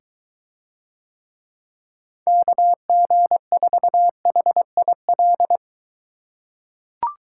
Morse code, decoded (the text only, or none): KG4HIL